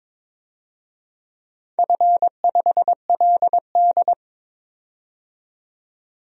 Morse code, heard as a keyed tone, about 22 wpm, 700 Hz. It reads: F5LD
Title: DXCC-100 Call Signs - 4 Characters - 22wpm